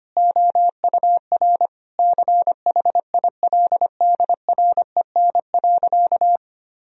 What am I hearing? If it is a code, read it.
Morse code, heard as a keyed tone, about 25 wpm, 700 Hz. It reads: OUR CHILDREN.
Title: Sentences from Top 1000 Words - Review Part 4 - 25wpm